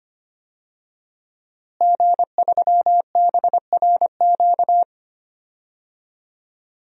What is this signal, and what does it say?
Morse code, heard as a keyed tone, about 25 wpm, 700 Hz. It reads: G3BRQ
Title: DXCC-100 Call Signs - 5 Characters - Repeated 2x - 25wpm